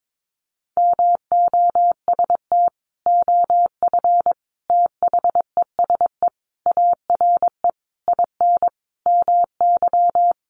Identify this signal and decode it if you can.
Morse code, heard as a keyed tone, about 22 wpm, 700 Hz. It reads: MOST OF THESE ARE IN MY
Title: Sentences from Top 1200 Words 22wpm